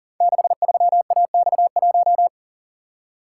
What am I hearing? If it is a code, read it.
Morse code, heard as a keyed tone, about 39 wpm, 700 Hz. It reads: 63AX1
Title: Code Groups - 5 Letter:Numbers - 40wpm